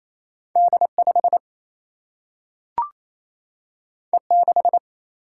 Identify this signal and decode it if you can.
Morse code, heard as a keyed tone, about 28 wpm, 700 Hz. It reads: D5 E6
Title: Single-Letter Single-Number 28wpm